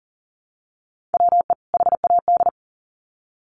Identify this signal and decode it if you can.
Morse code, heard as a keyed tone, about 40 wpm, 700 Hz. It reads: WE HAD
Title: Sets of 2 Words 40wpm